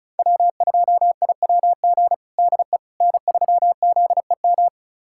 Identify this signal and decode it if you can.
Morse code, heard as a keyed tone, about 36 wpm, 700 Hz. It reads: W2IWG DE N3ZEM
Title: USA Callsign DE Callsign - 5 Characters - 35wpm